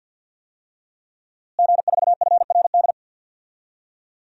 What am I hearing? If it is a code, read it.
Morse code, heard as a keyed tone, about 48 wpm, 700 Hz. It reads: G4FRD
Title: DXCC-100 Call Signs - 5 Characters - Repeated 2x - 50wpm